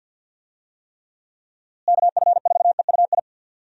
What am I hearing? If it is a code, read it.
Morse code, heard as a keyed tone, about 50 wpm, 700 Hz. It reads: KF4EUI
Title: USA Call Signs - 6 Characters - Repeated 2x - 50wpm